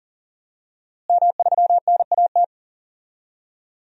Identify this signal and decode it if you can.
Morse code, heard as a keyed tone, about 38 wpm, 700 Hz. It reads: M3NAT
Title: DXCC-100 Call Signs - 5 Characters - Repeated 2x - 40wpm